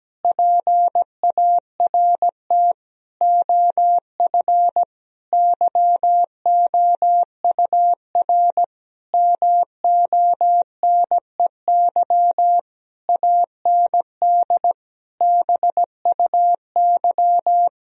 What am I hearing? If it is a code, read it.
Morse code, heard as a keyed tone, about 17 wpm, 700 Hz. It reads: PART OF YOUR MONEY AND BUY
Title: Sentences from Top 1000 Words - Review Part 4 - 17wpm